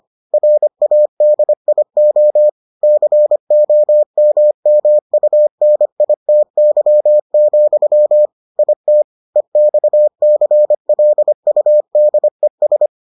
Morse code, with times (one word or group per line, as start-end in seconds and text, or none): 0.3-2.5 RADIO
2.8-8.3 COMMUNITY,
8.6-9.0 IT
9.4-12.9 EXCLUDES